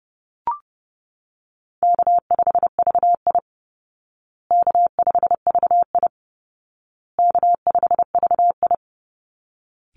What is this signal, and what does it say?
Morse code, heard as a keyed tone, about 30 wpm, 700 Hz. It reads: K5VI K5VI K5VI
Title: Call Signs Repeated 2x - 30wpm